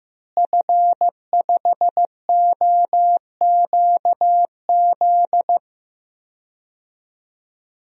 Morse code, heard as a keyed tone, about 15 wpm, 700 Hz. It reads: F5OQZ
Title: DXCC-100 Call Signs - 5 Characters - Repeated 2x - 15wpm